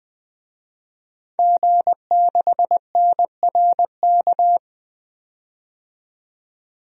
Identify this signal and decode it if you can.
Morse code, heard as a keyed tone, about 21 wpm, 700 Hz. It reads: G6NRK
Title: DXCC-100 Call Signs - 5 Characters - 20wpm